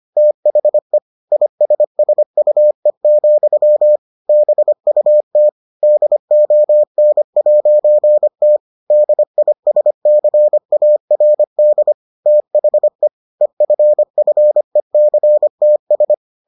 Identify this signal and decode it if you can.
Morse code, heard as a keyed tone, about 25 wpm, 600 Hz. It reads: THE ISSUE, BUT DON'T DISCARD THE EFFECTS